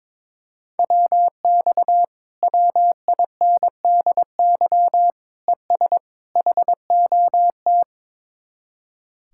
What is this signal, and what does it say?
Morse code, heard as a keyed tone, about 22 wpm, 700 Hz. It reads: WX WINDY ES HOT
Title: Speed-Racing Common QSO Phrases Encore - 22wpm